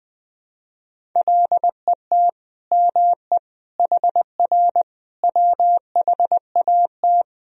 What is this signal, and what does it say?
Morse code, heard as a keyed tone, about 20 wpm, 700 Hz. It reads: LET ME HR WHAT
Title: Sentences from Top 300 Words and Abbreviations 20wpm